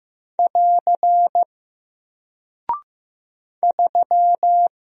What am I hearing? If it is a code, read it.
Morse code, heard as a keyed tone, about 15 wpm, 700 Hz. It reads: <AR> 3
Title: Single Character (Letter:Number:Punctuation:Prosign) Alt 15wpm